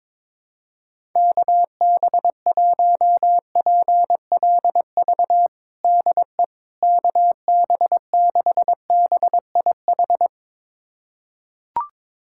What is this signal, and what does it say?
Morse code, heard as a keyed tone, about 22 wpm, 700 Hz. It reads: KB1PLV DE KB6BIH
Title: USA Callsign DE Callsign - 6 Characters - 22wpm